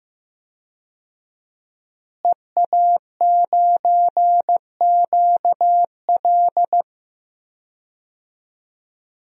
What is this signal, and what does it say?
Morse code, heard as a keyed tone, about 15 wpm, 700 Hz. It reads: EA9QL
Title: DXCC-100 Call Signs - 5 Characters - Repeated 3x - 15wpm